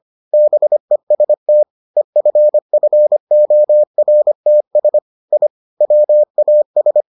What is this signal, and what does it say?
Morse code, heard as a keyed tone, about 25 wpm, 600 Hz. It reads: BEST EFFORTS I WAS